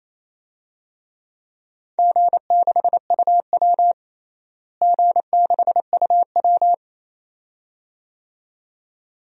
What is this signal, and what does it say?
Morse code, heard as a keyed tone, about 28 wpm, 700 Hz. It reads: G6UW G6UW
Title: DXCC-100 Call Signs - 4 Characters - Repeated 2x - 28wpm